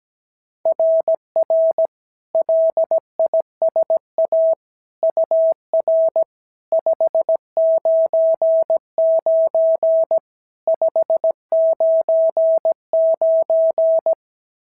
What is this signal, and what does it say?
Morse code, heard as a keyed tone, about 16 wpm, 650 Hz. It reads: RR LISA UR 599 599
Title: SOTA QSO Practice 17wpm